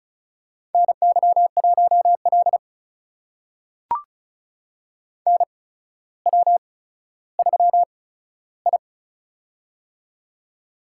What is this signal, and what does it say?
Morse code, heard as a keyed tone, about 36 wpm, spaced as for 15 wpm, 700 Hz. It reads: NY1L NW3I
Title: USA Call Signs - 4 Characters - 8x Character Spacing - 35wpm